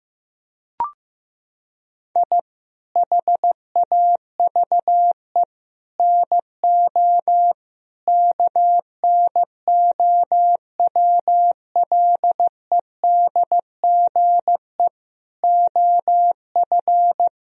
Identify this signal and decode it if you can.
Morse code, heard as a keyed tone, about 15 wpm, 700 Hz. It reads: I HAVE NO KNOWLEDGE OF
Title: Sentences from Top 1000 Words - Review Part 3 - 15wpm